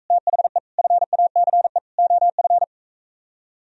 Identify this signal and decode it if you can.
Morse code, heard as a keyed tone, about 44 wpm, 700 Hz. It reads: THE FACE OF